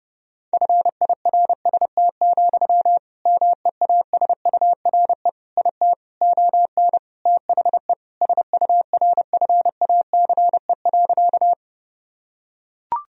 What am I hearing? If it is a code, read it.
Morse code, heard as a keyed tone, about 30 wpm, 700 Hz. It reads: FIRST, MEASURE IT ON THE SURFACE.
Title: Sentences from Top 1000 Words - Review Part 3 - 30wpm